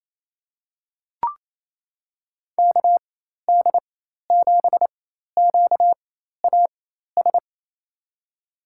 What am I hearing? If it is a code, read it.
Morse code, heard as a keyed tone, about 28 wpm, spaced as for 13 wpm, 700 Hz. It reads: KD7QAS
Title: USA Call Signs - 6 Characters - 4x Character Spacing - 28wpm